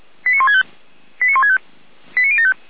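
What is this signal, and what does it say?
Five-tone selective call (ZVEI), heard as quick runs of five tones, tones of about 70 ms. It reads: 89165 89165 89075
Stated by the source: ZVEI five-tone selective call received on SDR.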